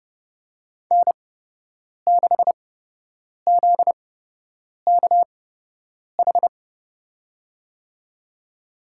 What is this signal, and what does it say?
Morse code, heard as a keyed tone, about 31 wpm, spaced as for 8 wpm, 700 Hz. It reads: N6ZKH
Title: USA Call Signs - 5 Characters - 8x Character Spacing - 30wpm